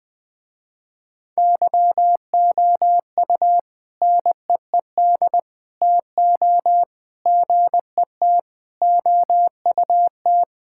Morse code, heard as a keyed tone, about 20 wpm, 700 Hz. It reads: YOU NEED TO GET OUT
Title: Sentences with 3-Word Phrases - Spoken Beginning - 20wpm